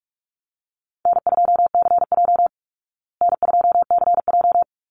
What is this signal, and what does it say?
Morse code, heard as a keyed tone, about 44 wpm, 700 Hz. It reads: N2CJ N2CJ
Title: Call Signs Repeated 3x - 45wpm